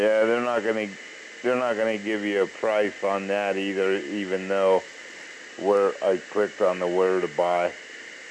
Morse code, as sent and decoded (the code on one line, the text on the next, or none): none